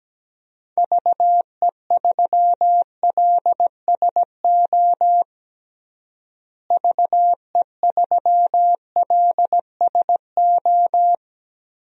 Morse code, heard as a keyed tone, about 17 wpm, 700 Hz. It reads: VE3LSO VE3LSO
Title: DXCC-100 Call Signs - 6 Characters - Repeated 2x - 17wpm